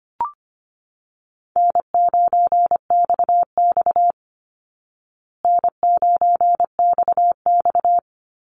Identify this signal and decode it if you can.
Morse code, heard as a keyed tone, about 25 wpm, 700 Hz. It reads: N9XX N9XX
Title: Easy Call Signs Repeated 3x - 25wpm